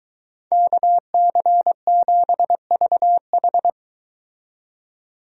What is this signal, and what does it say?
Morse code, heard as a keyed tone, about 23 wpm, 700 Hz. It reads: KC7VH